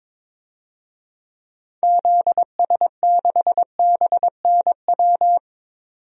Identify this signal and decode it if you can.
Morse code, heard as a keyed tone, about 22 wpm, 700 Hz. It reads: ZS6BNW
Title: DXCC-100 Call Signs - 6 Characters - 22wpm